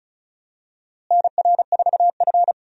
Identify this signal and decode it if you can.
Morse code, heard as a keyed tone, about 34 wpm, 700 Hz. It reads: NR4F
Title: USA Call Signs - 4 Characters - Repeated 2x - 35wpm